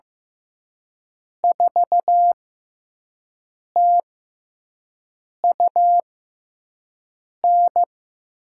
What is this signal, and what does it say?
Morse code, heard as a keyed tone, about 15 wpm, spaced as for 5 wpm, 700 Hz. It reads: 4TUN
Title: USA Call Signs - 5 Characters - 6x Character Spacing - 15wpm